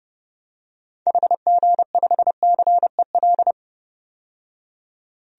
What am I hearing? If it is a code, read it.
Morse code, heard as a keyed tone, about 32 wpm, 700 Hz. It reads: HG5CEL
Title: DXCC-100 Call Signs - 6 Characters - 30wpm